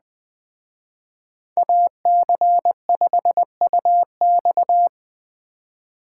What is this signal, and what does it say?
Morse code, heard as a keyed tone, about 19 wpm, 700 Hz. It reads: AC5UX
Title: Speed Racing - USA 2x2 Call Signs 17wpm